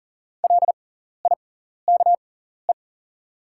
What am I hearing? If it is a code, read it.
Morse code, heard as a keyed tone, about 40 wpm, spaced as for 14 wpm, 700 Hz. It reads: LIKE
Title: 6x Character Spacing - Top 100 Words - 40wpm